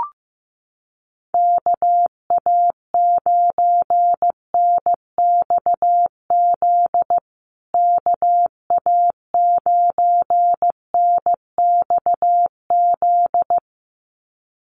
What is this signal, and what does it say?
Morse code, heard as a keyed tone, about 15 wpm, 700 Hz. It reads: KA9NXZ KA9NXZ
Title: USA Call Signs Repeated 2x 15wpm